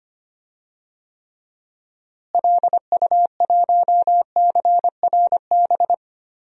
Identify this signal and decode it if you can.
Morse code, heard as a keyed tone, about 25 wpm, 700 Hz. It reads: LU1CRB
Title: DXCC-100 Call Signs - 6 Characters - Repeated 2x - 25wpm